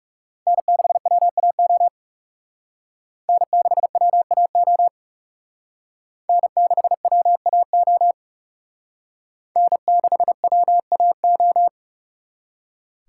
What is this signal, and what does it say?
Morse code, heard as a keyed tone, about 38 wpm, 700 Hz. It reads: N6WAO N6WAO N6WAO N6WAO
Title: Speed Racing - USA Call Signs 30wpm